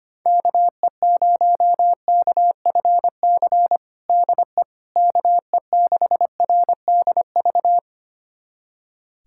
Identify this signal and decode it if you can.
Morse code, heard as a keyed tone, about 25 wpm, 700 Hz. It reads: KE0KFC DE KE6RDV
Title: USA Callsign DE Callsign - 6 Characters - 25wpm